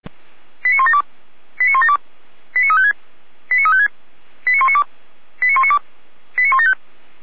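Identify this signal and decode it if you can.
Five-tone selective call (ZVEI), heard as quick runs of five tones, tones of about 80 ms. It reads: 89172 89172 89356 89356 89192 89192 89175
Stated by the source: Received on an SDR; ZVEI five-tone selective call.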